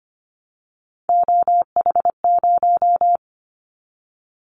Morse code, heard as a keyed tone, about 25 wpm, 700 Hz. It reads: OH0
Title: Callsign Prefixes for Top 100 DXCC - 25wpm